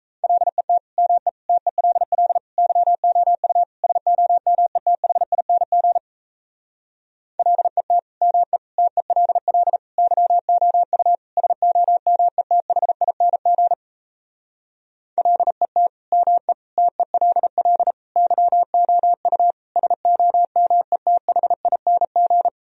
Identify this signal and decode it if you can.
Morse code, heard as a keyed tone, about 37 wpm, 700 Hz. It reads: LET ME TELL YOU SOMETHING LET ME TELL YOU SOMETHING LET ME TELL YOU SOMETHING